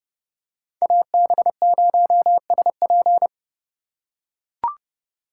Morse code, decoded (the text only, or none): AB0SP